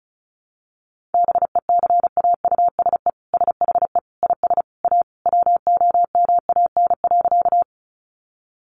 Morse code, heard as a keyed tone, about 34 wpm, 700 Hz. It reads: BECAUSE SHE IS A WOMAN.
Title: Sentences from Top 1100 Words 35wpm